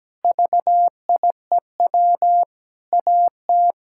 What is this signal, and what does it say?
Morse code, heard as a keyed tone, about 17 wpm, 700 Hz. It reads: VIEW AT
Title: Sentences from Top 1000 Words - Review Part 6 - 17wpm